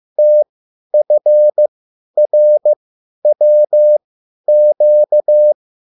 Morse code, heard as a keyed tone, about 15 wpm, spaced as for 10 wpm, 600 Hz. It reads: TFRWQ